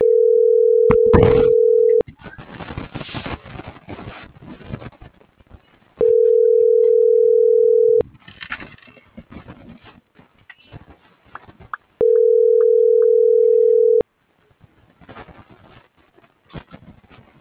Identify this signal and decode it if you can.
Ringback tone (North America), 440 + 480 Hz.